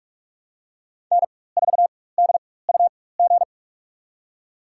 Morse code, heard as a keyed tone, about 46 wpm, spaced as for 21 wpm, 700 Hz. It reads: N4DUG
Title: USA Call Signs - 4x Character Spacing - 45wpm